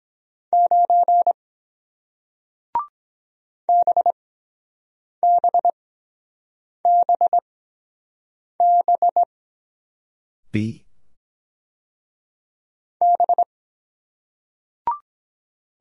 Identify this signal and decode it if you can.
Morse code, heard as a keyed tone, about 23 wpm, spaced as for 7 wpm, 700 Hz. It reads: 9 BBBB B